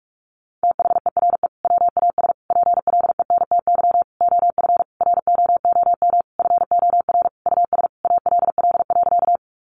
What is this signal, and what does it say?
Morse code, heard as a keyed tone, about 44 wpm, 700 Hz. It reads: THERE WAS PLENTY OF ROOM FOR US ALL.
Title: Sentences from Top 1300 Words 45wpm